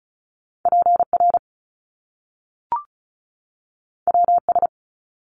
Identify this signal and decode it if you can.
Morse code, heard as a keyed tone, about 36 wpm, 700 Hz. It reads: PR WS
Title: Top Two Letter Combinations in Words 35wpm